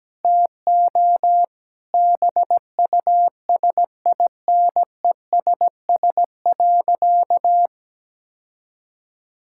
Morse code, heard as a keyed tone, about 17 wpm, 700 Hz. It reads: TO BUSINESS.
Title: Sentences from Top 1000 Words - Review Part 6 - 17wpm